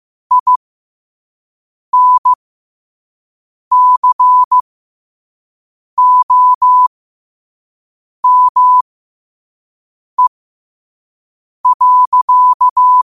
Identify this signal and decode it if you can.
Morse code, heard as a keyed tone, about 15 wpm, spaced as for 5 wpm, 1000 Hz. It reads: INCOME.
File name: 5wpm